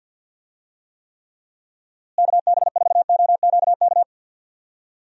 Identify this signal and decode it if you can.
Morse code, heard as a keyed tone, about 50 wpm, 700 Hz. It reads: KB4OQK
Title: USA Call Signs - 6 Characters - Repeated 3x - 50wpm